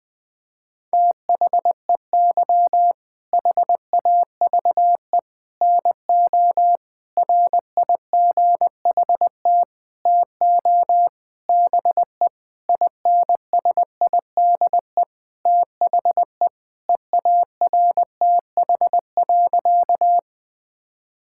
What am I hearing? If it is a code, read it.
Morse code, heard as a keyed tone, about 19 wpm, 700 Hz. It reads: THEY HAVE NO RIGHT TO BE INSIDE THE EARTH.
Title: Sentences from Top 1000 Words - Review Part 6 - 20wpm